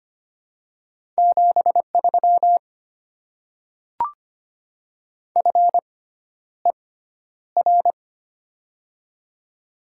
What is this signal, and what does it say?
Morse code, heard as a keyed tone, about 25 wpm, spaced as for 11 wpm, 700 Hz. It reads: 73 FER
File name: Top 20 QSO Elements - 6x Character Spacing - 25wpm